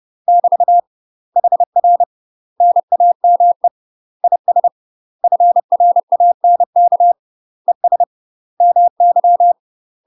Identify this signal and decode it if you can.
Morse code, heard as a keyed tone, about 30 wpm, 700 Hz. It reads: <BT> HR NAME IS FRANK ES MY
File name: UK MASTER QSOs TRAINING / 30WPM / UK MASTER QSOs 0009